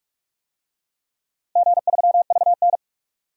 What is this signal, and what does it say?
Morse code, heard as a keyed tone, about 43 wpm, 700 Hz. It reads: G3VN
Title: DXCC-100 Call Signs - 4 Characters - Repeated 2x - 45wpm